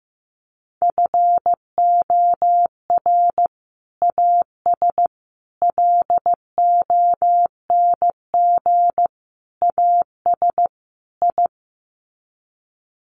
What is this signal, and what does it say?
Morse code, heard as a keyed tone, about 15 wpm, 700 Hz. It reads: FOR AS LONG AS I
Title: Sets of 5 Words Encore2 15wpm